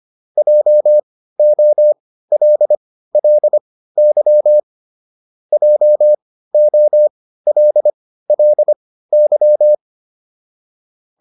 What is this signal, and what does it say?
Morse code, heard as a keyed tone, about 25 wpm, spaced as for 15 wpm, 600 Hz. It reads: JOLLY JOLLY